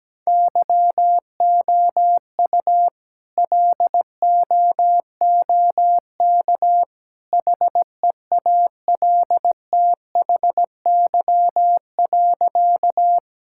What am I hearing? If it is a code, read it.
Morse code, heard as a keyed tone, about 17 wpm, 700 Hz. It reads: YOU LOOK HEALTHY.
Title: Sentences from Top 1000 Words - Review Part 5 - 17wpm